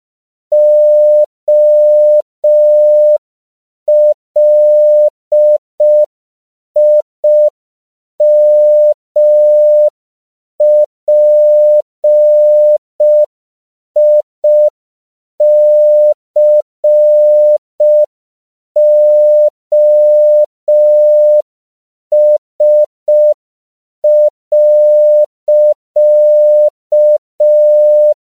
Morse code, heard as a keyed tone, about 5 wpm, 600 Hz. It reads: OLIMPICOS.
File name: CW / 5PPM